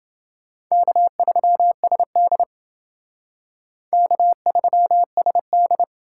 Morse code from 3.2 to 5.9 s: K3SD

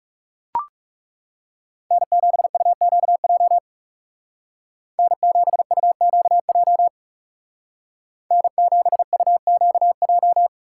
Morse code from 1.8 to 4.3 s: N7UQJ

